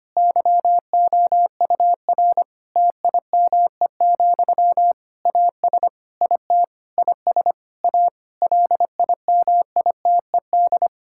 Morse code, heard as a keyed tone, about 25 wpm, 700 Hz. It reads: YOUR TIME, AS IT IS A LIMITED
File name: practice / 25wpm / Difficult Sentences with Hints - 25wpm